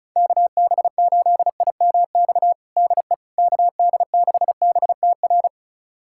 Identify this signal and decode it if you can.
Morse code, heard as a keyed tone, about 34 wpm, 700 Hz. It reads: KB8IMX DE KD6BTR
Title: USA Callsign DE Callsign 35wpm